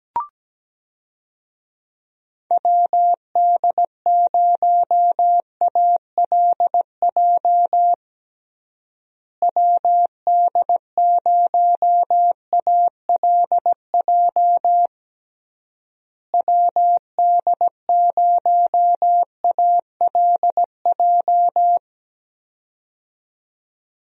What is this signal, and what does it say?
Morse code, heard as a keyed tone, about 17 wpm, 700 Hz. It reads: WD0ALJ WD0ALJ WD0ALJ
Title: USA Call Signs - 6 Characters - Repeated 3x - 17wpm